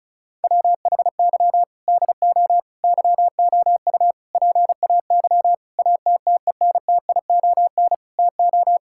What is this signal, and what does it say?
Morse code, heard as a keyed tone, about 36 wpm, 700 Hz. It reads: WHY DO YOU PAY ATTENTION TO
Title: Sentences from Top 1000 Words - Review Part 6 - 35wpm